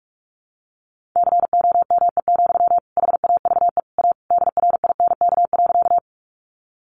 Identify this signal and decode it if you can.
Morse code, heard as a keyed tone, about 44 wpm, 700 Hz. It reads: COME, HAVE A DRINK.
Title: Sentences from Top 1500 Words 45wpm